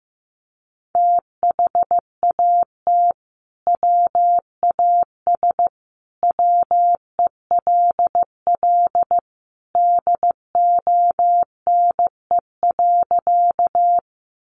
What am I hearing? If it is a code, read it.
Morse code, heard as a keyed tone, about 15 wpm, 700 Hz. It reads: THAT WAS WELL DONE.
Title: Sentences from Top 500 Words 15wpm